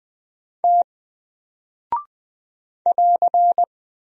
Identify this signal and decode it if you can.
Morse code, heard as a keyed tone, about 20 wpm, 700 Hz. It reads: T <AR>
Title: Single Character (Letter:Number:Punctuation:Prosign) 20wpm